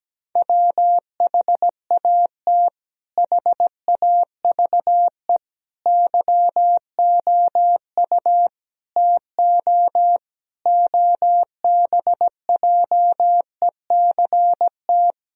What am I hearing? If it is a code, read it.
Morse code, heard as a keyed tone, about 17 wpm, 700 Hz. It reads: WHAT HAVE YOU TO OBJECT
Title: Sentences from Top 1000 Words - Review Part 3 - 17wpm